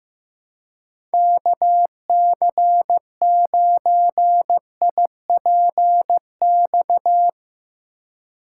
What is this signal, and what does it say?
Morse code, heard as a keyed tone, about 15 wpm, 700 Hz. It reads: KC9IPX
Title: USA Call Signs - 6 Characters - 4x Character Spacing - 15wpm